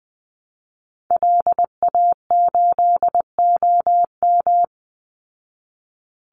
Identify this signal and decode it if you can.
Morse code, heard as a keyed tone, about 20 wpm, 700 Hz. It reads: LA8OM
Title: Call Signs Repeated 3x - 20wpm